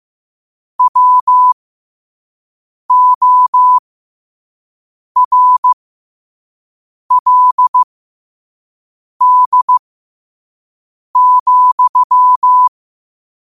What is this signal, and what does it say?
Morse code, heard as a keyed tone, about 15 wpm, spaced as for 5 wpm, 1000 Hz. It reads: WORLD,